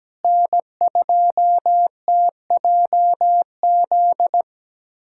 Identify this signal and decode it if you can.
Morse code, heard as a keyed tone, about 17 wpm, 700 Hz. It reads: N2TJZ
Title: USA Callsign DE Callsign 17wpm